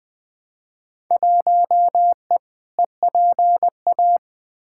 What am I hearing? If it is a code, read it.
Morse code, heard as a keyed tone, about 20 wpm, 700 Hz. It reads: 1E EPA
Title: Field Day Exchanges - 20wpm